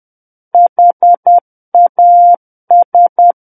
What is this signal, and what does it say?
Morse code, heard as a keyed tone, about 10 wpm, 700 Hz. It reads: HAS
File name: Book One / 01-Chapter 1 - 10 WPM